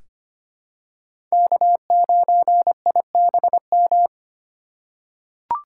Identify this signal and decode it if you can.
Morse code, heard as a keyed tone, about 25 wpm, 700 Hz. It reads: K9IBM
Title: K1USN SST QSO Practice - 25wpm